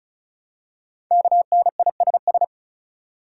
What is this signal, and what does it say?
Morse code, heard as a keyed tone, about 36 wpm, 700 Hz. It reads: KNISS
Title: Code Groups - 5 Letters - 35wpm